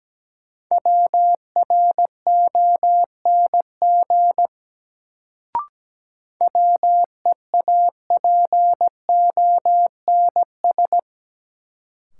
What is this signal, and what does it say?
Morse code, heard as a keyed tone, about 17 wpm, 700 Hz. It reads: WRONG WEAPONS
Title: Sets of 5 Words Encore - New Word Drill - 17wpm